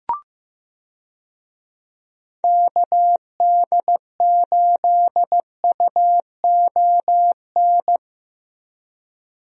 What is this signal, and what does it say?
Morse code, heard as a keyed tone, about 15 wpm, 700 Hz. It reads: KD8UON
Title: USA Call Signs - 6 Characters - Repeated 2x - 15wpm